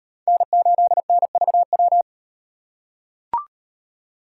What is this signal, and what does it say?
Morse code, heard as a keyed tone, about 38 wpm, 700 Hz. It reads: N8NVW